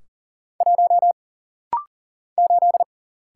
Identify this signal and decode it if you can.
Morse code, heard as a keyed tone, about 40 wpm, 700 Hz. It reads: 1 8
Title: Single Letter-Number - Rapid-Fire 40wpm